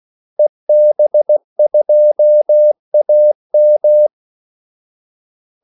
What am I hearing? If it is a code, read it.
Morse code, heard as a keyed tone, about 16 wpm, 600 Hz. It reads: EB2AM
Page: Kaldesignaler 16 wpm